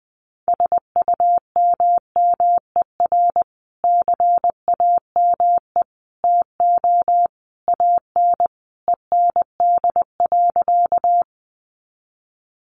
Morse code, heard as a keyed tone, about 20 wpm, 700 Hz. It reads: SUMMER CAME TO AN END.
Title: Sentences from Top 1100 Words 20wpm